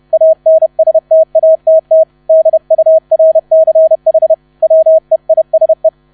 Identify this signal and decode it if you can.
Morse code, heard as a keyed tone, about 31 wpm, 640 Hz. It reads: ANSTATT DURCH WEISE
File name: 30wpm